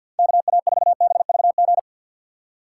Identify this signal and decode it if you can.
Morse code, heard as a keyed tone, about 52 wpm, 700 Hz. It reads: KA4BVG
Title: K1USN SST QSO Practice - 50wpm